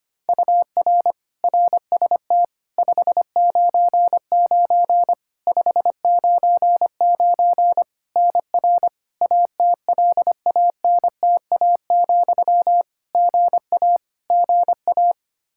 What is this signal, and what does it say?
Morse code, heard as a keyed tone, about 25 wpm, 700 Hz. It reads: UR RST 599 599 NR ATLANTA, GA GA
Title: Practice QSO - Exchange 1 - Beginning - 25wpm